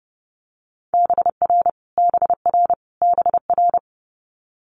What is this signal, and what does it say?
Morse code, heard as a keyed tone, about 30 wpm, 700 Hz. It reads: BR BR BR
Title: Multi-letter Phonemes Repeated 3x 30wpm